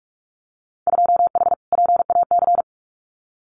Morse code, heard as a keyed tone, about 44 wpm, 690 Hz. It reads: 2H PAC
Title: Winter Field Day Exchanges Encore 45wpm